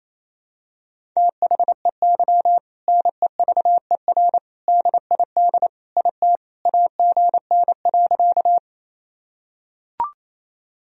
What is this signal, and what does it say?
Morse code, heard as a keyed tone, about 28 wpm, 700 Hz. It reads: THEY NEVER DID IT AGN.
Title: Sentences from Top 300 Words and Abbreviations 28wpm